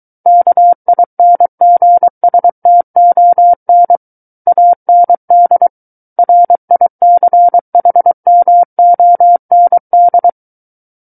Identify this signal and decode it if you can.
Morse code, heard as a keyed tone, about 23 wpm, 700 Hz. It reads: KINGSTON AND RICHMOND
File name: Book One / 14-Chapter 14 - 23 WPM